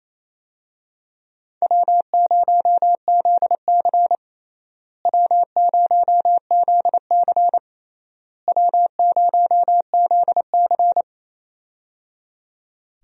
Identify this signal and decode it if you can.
Morse code, heard as a keyed tone, about 28 wpm, 700 Hz. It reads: W0ZC W0ZC W0ZC
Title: USA Call Signs - 4 Characters - Repeated 3x - 28wpm